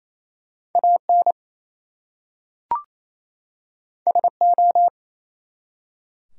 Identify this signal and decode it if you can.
Morse code, heard as a keyed tone, about 27 wpm, 700 Hz. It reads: AN SO
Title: Two Letter Words - 28wpm